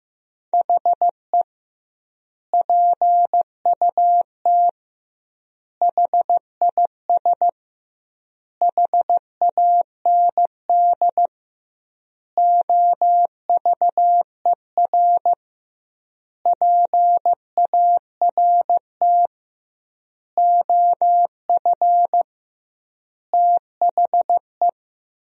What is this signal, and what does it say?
Morse code, heard as a keyed tone, about 15 wpm, 700 Hz. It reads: HE PUT HIS HAND OVER PART OF THE
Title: 2x Word-Spacing - Sentences from Top 400 Words - 15wpm